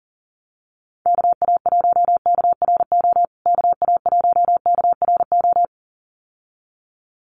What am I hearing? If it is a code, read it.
Morse code, heard as a keyed tone, about 37 wpm, 700 Hz. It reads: KA1KRO KA1KRO
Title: USA Call Signs Repeated 2x 40wpm